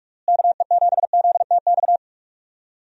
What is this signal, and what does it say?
Morse code, heard as a keyed tone, about 46 wpm, 700 Hz. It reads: KE7ZTX
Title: USA Call Signs - 6 Characters - Repeated 3x - 45wpm